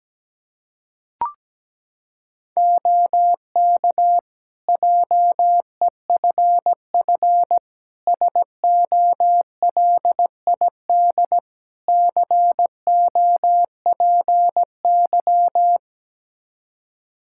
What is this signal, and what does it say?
Morse code, heard as a keyed tone, about 17 wpm, 700 Hz. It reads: OK JEFF SOLID COPY
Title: K1USN SST QSO Practice 17wpm